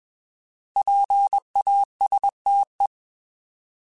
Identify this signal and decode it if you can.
Morse code, heard as a keyed tone, about 21 wpm, 800 Hz. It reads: PASTE
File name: Alphabetical / Alphabetical mp3s / Ps @20